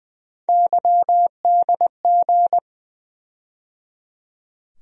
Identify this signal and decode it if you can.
Morse code, heard as a keyed tone, about 20 wpm, 700 Hz. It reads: YDG
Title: USA Call Sign Postfixes - 3 Letters 20wpm